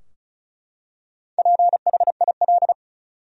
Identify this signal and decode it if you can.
Morse code, heard as a keyed tone, about 36 wpm, 700 Hz. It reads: PHIL
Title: USA Common Names - Max 4 Letters - 35wpm